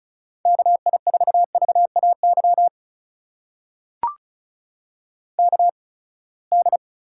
Morse code, heard as a keyed tone, about 36 wpm, spaced as for 24 wpm, 700 Hz. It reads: KI4VAY KD
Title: USA Call Signs - 6 Characters - 8x Character Spacing - 35wpm